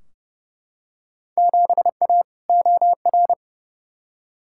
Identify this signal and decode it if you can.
Morse code, heard as a keyed tone, about 32 wpm, 700 Hz. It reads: 7A OR
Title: Field Day Exchanges Encore - 30wpm